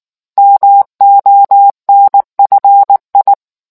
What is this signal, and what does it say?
Morse code, heard as a keyed tone, about 19 wpm, 800 Hz. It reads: MONFI